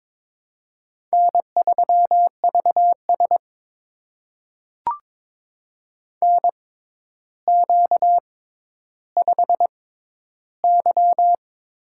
Morse code, heard as a keyed tone, about 22 wpm, spaced as for 11 wpm, 700 Hz. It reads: N3VS NQ5Y
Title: USA Call Signs - 4 Characters - 6x Character Spacing - 22wpm